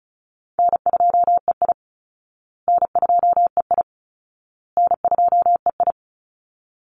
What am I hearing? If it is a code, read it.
Morse code, heard as a keyed tone, about 34 wpm, 700 Hz. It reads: N2EI N2EI N2EI
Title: Easy Call Signs Repeated 3x - 35wpm